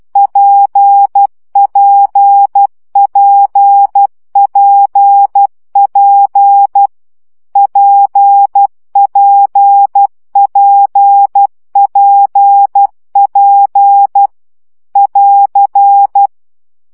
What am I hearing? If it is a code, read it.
Morse code, heard as a keyed tone, about 12 wpm, 800 Hz. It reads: PPPPP PPPPP <AR>
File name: CW gyakorlas p 10 baud